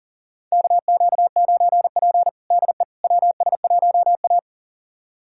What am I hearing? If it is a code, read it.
Morse code, heard as a keyed tone, about 40 wpm, 700 Hz. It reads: KQ9P DE WS1A
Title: USA Callsign DE Callsign - 4 Characters - 40wpm